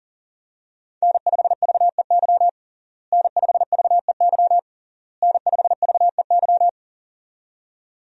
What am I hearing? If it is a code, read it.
Morse code, heard as a keyed tone, about 37 wpm, 700 Hz. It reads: N5VEY N5VEY N5VEY
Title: USA Call Signs - 5 Characters - Repeated 3x - 40wpm